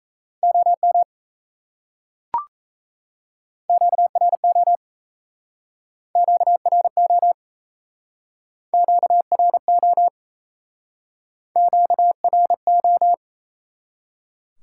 Morse code, heard as a keyed tone, about 37 wpm, 700 Hz. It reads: OM QRO QRO QRO QRO